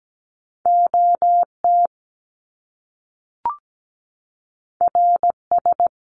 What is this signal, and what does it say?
Morse code, heard as a keyed tone, about 17 wpm, 700 Hz. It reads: OT RS
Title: Top Two Letter Combinations in Words 17wpm